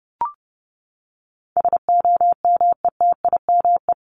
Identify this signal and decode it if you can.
Morse code, heard as a keyed tone, about 30 wpm, 700 Hz. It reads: SOMETIME
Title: ES:S Suffix - Top 100 Words with Base Words 30wpm